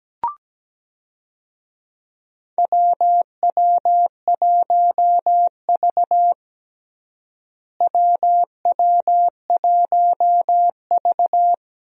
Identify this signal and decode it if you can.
Morse code, heard as a keyed tone, about 17 wpm, 700 Hz. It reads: WW1V WW1V
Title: USA Call Signs - 4 Characters - Repeated 2x - 17wpm